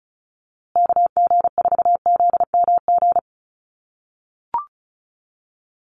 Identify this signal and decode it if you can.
Morse code, heard as a keyed tone, about 34 wpm, 700 Hz. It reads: KG4ZMG
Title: USA Call Signs Repeated 2x 35wpm